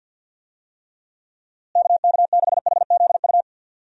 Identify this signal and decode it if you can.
Morse code, heard as a keyed tone, about 50 wpm, 700 Hz. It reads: KK6HZU
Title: USA Call Signs - 6 Characters - Repeated 3x - 50wpm